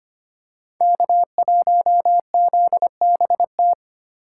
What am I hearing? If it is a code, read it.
Morse code, heard as a keyed tone, about 25 wpm, 700 Hz. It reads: K1ZBT
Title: USA Call Signs - 5 Characters - 3x Character Spacing - 25wpm